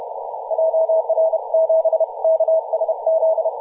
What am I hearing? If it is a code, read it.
Morse code, heard as a keyed tone, about 32 wpm, 650 Hz, with radio noise. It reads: JR7KSZ
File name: t2ar-14-027mhz-cw